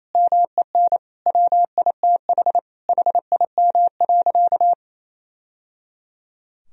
Morse code, heard as a keyed tone, about 28 wpm, 700 Hz. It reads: MEN WITH HIM.
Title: Sentences from Top 1000 Words - Review Part 2 - 28wpm